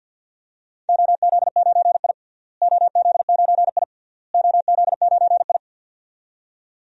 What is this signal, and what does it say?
Morse code, heard as a keyed tone, about 51 wpm, 700 Hz. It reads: OZ9I OZ9I OZ9I
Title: DXCC-100 Call Signs - 4 Characters - Repeated 3x - 50wpm